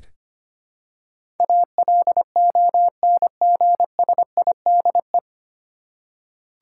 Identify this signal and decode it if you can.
Morse code, heard as a keyed tone, about 25 wpm, 700 Hz. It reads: ALONGSIDE
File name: Nine Letter Words - 25wpm